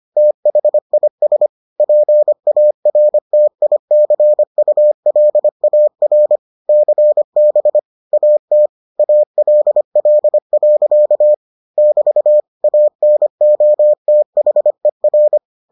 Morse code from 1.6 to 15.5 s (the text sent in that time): PARTICULAR CB AT ALL. <BT> ANOTHER